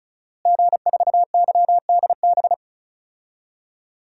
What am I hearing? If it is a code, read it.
Morse code, heard as a keyed tone, about 36 wpm, 700 Hz. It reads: G4YDB
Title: DXCC-100 Call Signs - 5 Characters - Repeated 2x - 35wpm